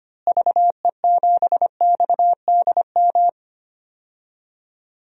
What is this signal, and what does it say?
Morse code, heard as a keyed tone, about 25 wpm, 700 Hz. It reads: VE7XDM
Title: DXCC-100 Call Signs - 6 Characters - Repeated 3x - 25wpm